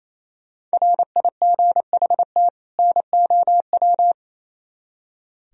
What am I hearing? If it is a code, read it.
Morse code, heard as a keyed tone, about 35 wpm, 700 Hz. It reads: RIGHT NOW
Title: Speed-Racing - Sets of 2 Words 28wpm